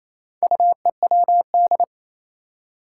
Morse code, heard as a keyed tone, about 28 wpm, 700 Hz. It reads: UEWD
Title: Four Challenging Letter-Number Combinations 28wpm